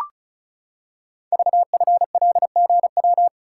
Morse code, heard as a keyed tone, about 36 wpm, 700 Hz. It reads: VFLGW
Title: Code Groups - 5 Letters - 35wpm